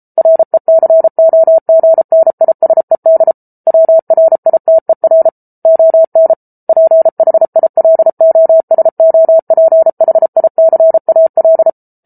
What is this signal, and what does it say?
Morse code, heard as a keyed tone, about 33 wpm, 650 Hz. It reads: RECOGNISED WRITER ON PHILOSOPHICAL